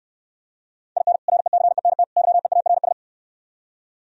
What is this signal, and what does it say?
Morse code, heard as a keyed tone, about 67 wpm, 700 Hz. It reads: IN RECENT YEARS